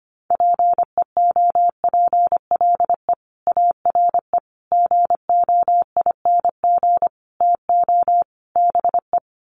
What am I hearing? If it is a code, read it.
Morse code, heard as a keyed tone, about 25 wpm, 700 Hz. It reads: PEOPLE ARE GOING TO BE
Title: Sets of 5 Words Encore 25wpm